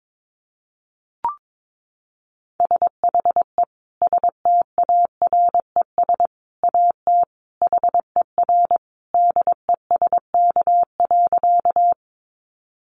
Morse code, heard as a keyed tone, about 22 wpm, 700 Hz. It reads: SHE STARES AT HER DESK.
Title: Easy Phrases 22wpm